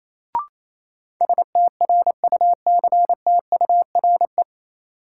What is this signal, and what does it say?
Morse code, heard as a keyed tone, about 28 wpm, 700 Hz. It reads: STRUCTURE
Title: Sentences from Top 1000 Words - Review Part 6 - New Word Drill - 28wpm